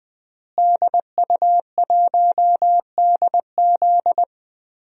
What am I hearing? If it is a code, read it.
Morse code, heard as a keyed tone, about 20 wpm, 700 Hz. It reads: DU1DZ
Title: DXCC-100 Call Signs - 5 Characters - Repeated 3x - 20wpm